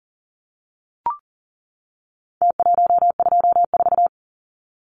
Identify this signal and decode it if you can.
Morse code, heard as a keyed tone, about 40 wpm, 700 Hz. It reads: T124
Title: Numbers - Contest Style - 4-Digit 40wpm